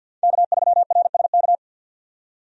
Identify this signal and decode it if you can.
Morse code, heard as a keyed tone, about 48 wpm, 700 Hz. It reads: K3RSK
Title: USA Call Signs - 5 Characters - 50wpm